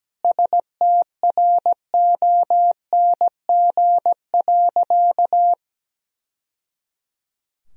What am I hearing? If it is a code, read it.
Morse code, heard as a keyed tone, about 17 wpm, 700 Hz. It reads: STRONG.
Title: Sentences from Top 1000 Words - Review Part 3 - 17wpm